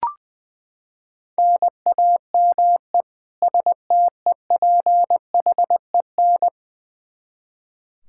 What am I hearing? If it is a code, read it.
Morse code, heard as a keyed tone, about 21 wpm, 700 Hz. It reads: NAME STEPHEN
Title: Practice QSO - Exchange 1 - Beginning - 20wpm